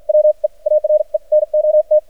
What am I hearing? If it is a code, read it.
Morse code, heard as a keyed tone, about 53 wpm, spaced as for 44 wpm, 610 Hz. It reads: WE ARE NOT
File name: archived / CRSnet